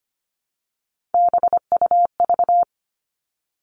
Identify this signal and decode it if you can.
Morse code, heard as a keyed tone, about 25 wpm, 700 Hz. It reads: BUV
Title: All Cut Numbers - 3-Digit 25wpm